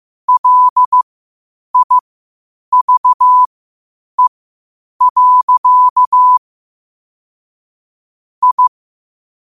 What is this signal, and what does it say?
Morse code, heard as a keyed tone, about 15 wpm, spaced as for 8 wpm, 1000 Hz. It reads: LIVE. I
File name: qotdfiles / 8wpm